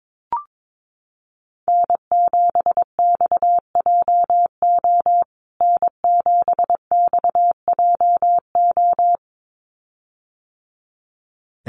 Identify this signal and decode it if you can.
Morse code, heard as a keyed tone, about 22 wpm, 700 Hz. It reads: N7XJO N7XJO
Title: USA Call Signs Repeated 2x 22wpm